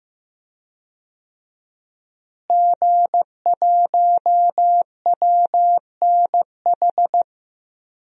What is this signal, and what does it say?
Morse code, heard as a keyed tone, about 16 wpm, 700 Hz. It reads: G1WNH
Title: DXCC-100 Call Signs - 5 Characters - Repeated 3x - 15wpm